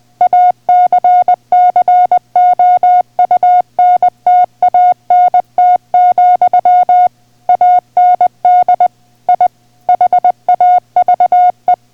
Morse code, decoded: ACCOUNTANT, AND I HAVE